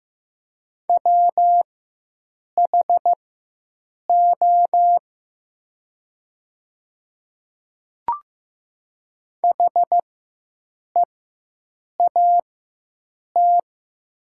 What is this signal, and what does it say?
Morse code, heard as a keyed tone, about 15 wpm, spaced as for 7 wpm, 700 Hz. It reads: WHO HEAT
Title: 4x Character Spacing - Top 500 Words - 15wpm